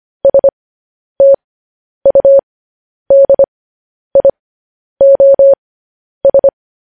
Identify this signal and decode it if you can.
Morse code, heard as a keyed tone, about 25 wpm, spaced as for 10 wpm, 550 Hz. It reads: STUDIOS
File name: podcast / News headlines-1x-25x10